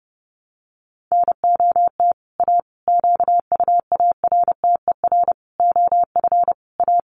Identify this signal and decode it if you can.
Morse code, heard as a keyed tone, about 28 wpm, 700 Hz. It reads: NOT A QUARTER OF A
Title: Sentences from Top 1300 Words 30wpm